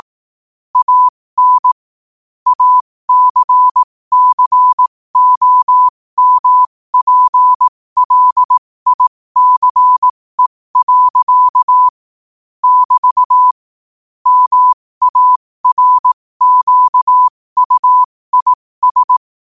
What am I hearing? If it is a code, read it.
Morse code, heard as a keyed tone, about 18 wpm, spaced as for 15 wpm, 1000 Hz. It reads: AN ACCOMPLICE. <BT> MARQUIS
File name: qotdfiles / 15wpm